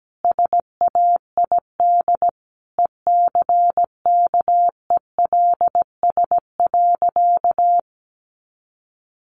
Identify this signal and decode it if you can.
Morse code, heard as a keyed tone, about 17 wpm, 700 Hz. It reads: SAID ECKELS.
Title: A Sound Of Thunder 17wpm